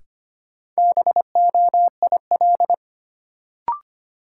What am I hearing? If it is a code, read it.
Morse code, heard as a keyed tone, about 25 wpm, 700 Hz. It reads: BOIL
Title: Sets of 5 Words Encore - New Word Drill - 25wpm